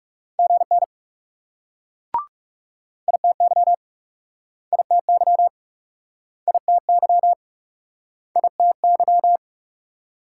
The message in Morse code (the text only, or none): GN ITY ITY ITY ITY